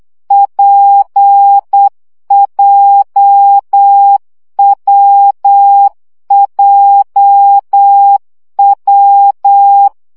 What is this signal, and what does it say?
Morse code, heard as a keyed tone, about 8 wpm, 800 Hz. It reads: PJWJW
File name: CW gyakorlas w p j 7 baud